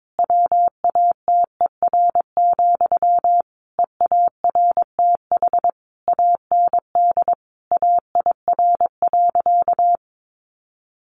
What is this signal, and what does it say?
Morse code, heard as a keyed tone, about 22 wpm, 700 Hz. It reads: WATER, EARTH AND AIR.
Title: Sentences from Top 400 Words 22wpm